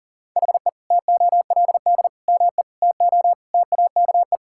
Morse code, heard as a keyed tone, about 40 wpm, 700 Hz. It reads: HE TOLD ME TO TAKE
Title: Sentences from Top 1000 Words - Review Part 6 - 40wpm